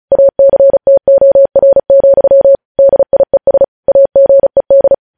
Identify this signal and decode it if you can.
Morse code, heard as a keyed tone, about 36 wpm, 550 Hz. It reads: ACTOR, DIES AGED
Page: News Headlines in Morse Code at 35 WPM